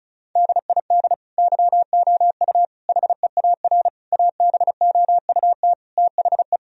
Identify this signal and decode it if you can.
Morse code, heard as a keyed tone, about 34 wpm, 700 Hz. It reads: DID YOU HEAR ABOUT THE
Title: Sentences from Top 1000 Words - Review Part 4 - 35wpm